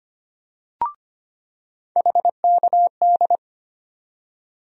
Morse code, heard as a keyed tone, about 25 wpm, 700 Hz. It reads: HKD
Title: USA Call Sign Postfixes - 3 Letters 25wpm